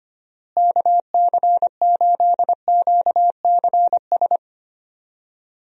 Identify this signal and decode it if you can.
Morse code, heard as a keyed tone, about 25 wpm, 700 Hz. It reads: KC8QCS
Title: USA Call Signs - 6 Characters - Repeated 3x - 25wpm